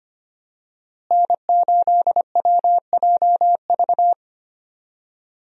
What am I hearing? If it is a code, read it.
Morse code, heard as a keyed tone, about 25 wpm, 700 Hz. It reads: N8WJV
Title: USA Call Signs - 5 Characters - Repeated 3x - 25wpm